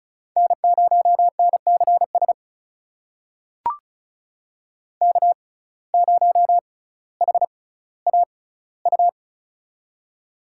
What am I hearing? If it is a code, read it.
Morse code, heard as a keyed tone, about 36 wpm, spaced as for 18 wpm, 700 Hz. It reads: N0NCS K0HAU
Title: USA Call Signs - 5 Characters - 6x Character Spacing - 35wpm